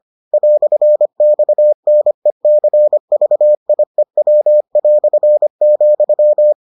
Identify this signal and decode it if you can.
Morse code, heard as a keyed tone, about 25 wpm, 600 Hz. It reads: "XNECVIEW",